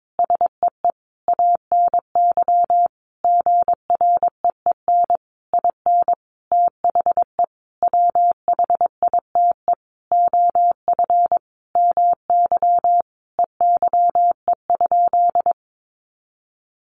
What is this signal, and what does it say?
Morse code, heard as a keyed tone, about 22 wpm, 700 Hz. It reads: SEE ANY GREEN IN THE WHITE OF MY EYE?
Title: Sentences from Top 500 Words 22wpm